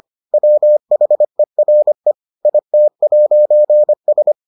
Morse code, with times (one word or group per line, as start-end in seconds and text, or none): 0.3-2.1 WHERE
2.5-4.3 IT'S